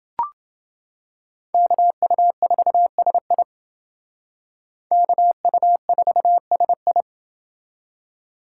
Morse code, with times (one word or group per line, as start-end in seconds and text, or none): 1.5-3.4 KU4SI
4.9-7.0 KU4SI